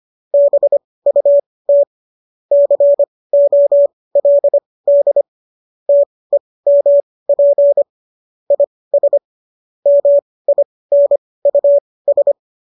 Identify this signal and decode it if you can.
Morse code, heard as a keyed tone, about 25 wpm, spaced as for 18 wpm, 570 Hz. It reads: BUT COLD TEMP IS MINUS
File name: qso203 18